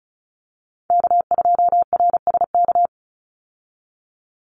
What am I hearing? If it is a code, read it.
Morse code, heard as a keyed tone, about 36 wpm, 700 Hz. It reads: K2RSK
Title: Call Signs - 35wpm